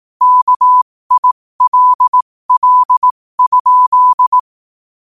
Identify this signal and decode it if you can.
Morse code, heard as a keyed tone, about 18 wpm, spaced as for 15 wpm, 1000 Hz. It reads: KILL?
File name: qotdfiles / 15wpm